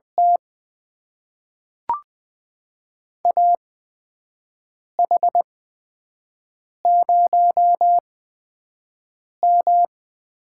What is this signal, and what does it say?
Morse code, heard as a keyed tone, about 20 wpm, spaced as for 5 wpm, 700 Hz. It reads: T AH0M
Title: USA Call Signs - 4 Characters - 8x Character Spacing - 20wpm